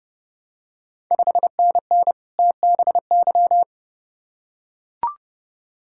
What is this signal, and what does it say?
Morse code, heard as a keyed tone, about 30 wpm, 700 Hz. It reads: 5NN TBY
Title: Ontario QSO Party Exchange 30wpm